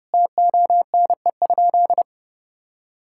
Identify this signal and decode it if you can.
Morse code, heard as a keyed tone, about 30 wpm, 700 Hz. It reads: TONE?
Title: Sentences from Top 1600 Words 30wpm